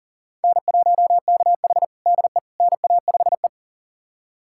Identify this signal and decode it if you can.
Morse code, heard as a keyed tone, about 40 wpm, 700 Hz. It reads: N1KH DE NA5E
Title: USA Callsign DE Callsign - 4 Characters - 40wpm